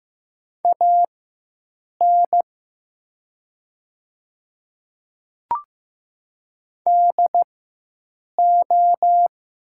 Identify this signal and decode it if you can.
Morse code, heard as a keyed tone, about 15 wpm, spaced as for 7 wpm, 700 Hz. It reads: AN DO
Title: 4x Character Spacing - Top 200 Words - 15wpm